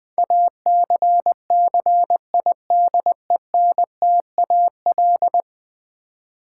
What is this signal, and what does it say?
Morse code, heard as a keyed tone, about 20 wpm, 700 Hz. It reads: ACCIDENTAL
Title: Build A Word - 5 Letters At A Time - 12-Letter Words - 20wpm